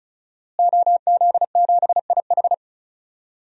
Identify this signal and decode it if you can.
Morse code, heard as a keyed tone, about 36 wpm, 700 Hz. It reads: OZ7IH
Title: DXCC-100 Call Signs - 5 Characters - Repeated 2x - 35wpm